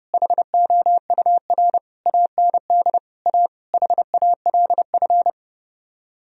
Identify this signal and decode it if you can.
Morse code, heard as a keyed tone, about 38 wpm, 700 Hz. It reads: HOUR AND A HALF